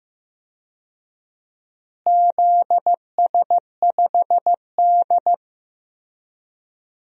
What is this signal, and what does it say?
Morse code, heard as a keyed tone, about 15 wpm, 700 Hz. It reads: ZS5D
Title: DXCC-100 Call Signs - 4 Characters - Repeated 3x - 15wpm